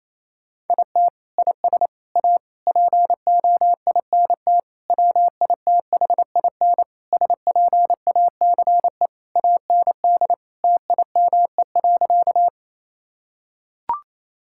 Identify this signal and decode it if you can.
Morse code, heard as a keyed tone, about 28 wpm, 700 Hz. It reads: IT IS A POINT WITHIN SPACE AND TIME.
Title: Sentences from Top 1000 Words - Review Part 5 - 28wpm